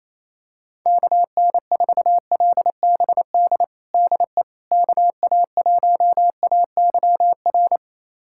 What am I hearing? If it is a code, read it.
Morse code, heard as a keyed tone, about 28 wpm, 700 Hz. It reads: KN4LBD DE KA1AYR
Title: USA Callsign DE Callsign - 6 Characters - 28wpm